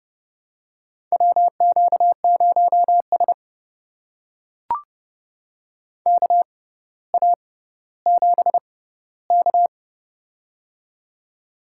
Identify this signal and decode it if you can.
Morse code, heard as a keyed tone, about 31 wpm, spaced as for 16 wpm, 700 Hz. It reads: WQ0S KA7K
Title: USA Call Signs - 4 Characters - 6x Character Spacing - 30wpm